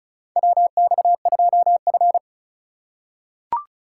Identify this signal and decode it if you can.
Morse code, heard as a keyed tone, about 36 wpm, 700 Hz. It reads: WX2F
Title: USA Call Signs - 4 Characters - 2x Character Spacing - 35wpm